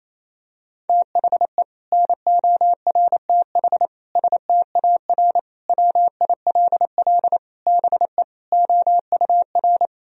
Morse code, heard as a keyed tone, about 28 wpm, 700 Hz. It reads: THE NORTH STAR WILL BE OUR
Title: Sentences from Top 1000 Words - Review Part 3 - 28wpm